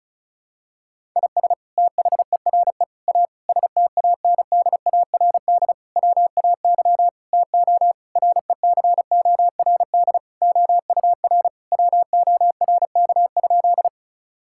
Morse code, heard as a keyed tone, about 36 wpm, 700 Hz. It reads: IS THERE A STANDARD WAY TO RECORD OUR WORK?
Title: Sentences from Top 1000 Words - Review Part 6 - 35wpm